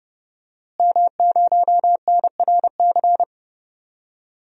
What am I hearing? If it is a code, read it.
Morse code, heard as a keyed tone, about 30 wpm, 700 Hz. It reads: M0NRC
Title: DXCC-100 Call Signs - 5 Characters - Repeated 3x - 30wpm